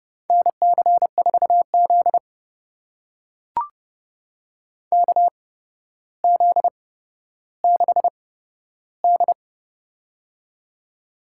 Morse code, heard as a keyed tone, about 30 wpm, spaced as for 13 wpm, 700 Hz. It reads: NC4Z KZ6D
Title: USA Call Signs - 4 Characters - 8x Character Spacing - 30wpm